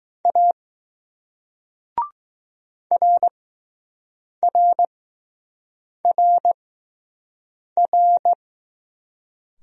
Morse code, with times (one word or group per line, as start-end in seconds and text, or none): 0.2-0.5 A
2.9-8.3 RRRR